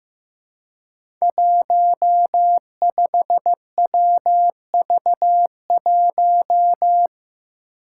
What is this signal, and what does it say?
Morse code, heard as a keyed tone, about 15 wpm, 700 Hz. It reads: 15WV1
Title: Five Challenging Letter-Number Combinations 15wpm